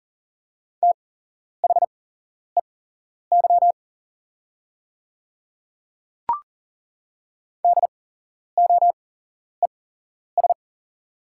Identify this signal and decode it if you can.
Morse code, heard as a keyed tone, about 40 wpm, spaced as for 11 wpm, 690 Hz. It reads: THEY DOES
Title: 8x Character Spacing - Top 200 Words - 40wpm